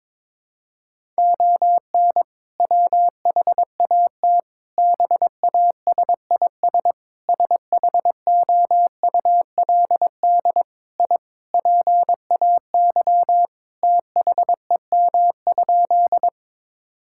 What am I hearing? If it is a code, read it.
Morse code, heard as a keyed tone, about 22 wpm, 700 Hz. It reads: ON WHAT BASIS SHOULD I PAY THEM?
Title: Sentences from Top 1000 Words - Review Part 5 - 22wpm